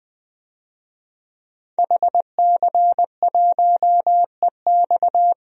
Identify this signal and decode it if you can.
Morse code, heard as a keyed tone, about 20 wpm, 700 Hz. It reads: HC1EX
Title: DXCC-100 Call Signs - 5 Characters - 20wpm